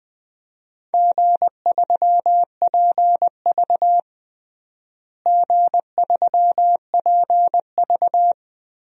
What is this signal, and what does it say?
Morse code, heard as a keyed tone, about 21 wpm, 700 Hz. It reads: G3PV G3PV
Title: DXCC-100 Call Signs - 4 Characters - Repeated 3x - 20wpm